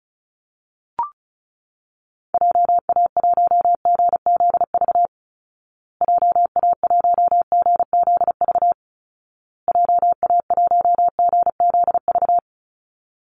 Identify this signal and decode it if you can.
Morse code, heard as a keyed tone, about 34 wpm, 700 Hz. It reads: JA1GZV JA1GZV JA1GZV
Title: Call Signs Repeated 2x - 35wpm